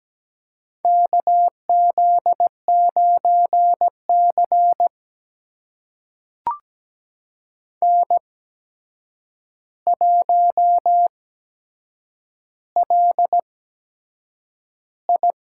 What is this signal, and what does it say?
Morse code, heard as a keyed tone, about 17 wpm, spaced as for 7 wpm, 700 Hz. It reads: KZ9C N1LI
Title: USA Call Signs - 4 Characters - 8x Character Spacing - 17wpm